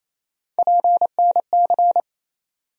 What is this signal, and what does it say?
Morse code, heard as a keyed tone, about 28 wpm, 700 Hz. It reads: PNC
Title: USA Call Sign Postfixes - 3 Letters 28wpm